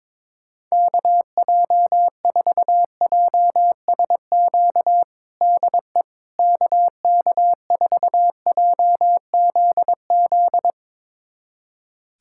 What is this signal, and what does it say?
Morse code, heard as a keyed tone, about 21 wpm, 700 Hz. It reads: KJ4JSQ DE KK4JZZ
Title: USA Callsign DE Callsign 22wpm